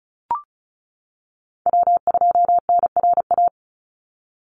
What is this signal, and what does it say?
Morse code, heard as a keyed tone, about 34 wpm, 700 Hz. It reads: W2NRA
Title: Call Signs - 35wpm